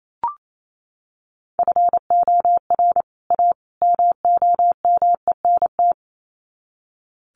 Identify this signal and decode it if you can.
Morse code, heard as a keyed tone, about 28 wpm, 700 Hz. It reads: FOR A MOMENT
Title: Sets of 3 Words 28wpm